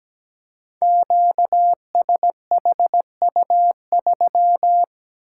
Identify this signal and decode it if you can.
Morse code, heard as a keyed tone, about 17 wpm, 700 Hz. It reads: QSHU3
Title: Code Groups - 5 Letter:Numbers - 17wpm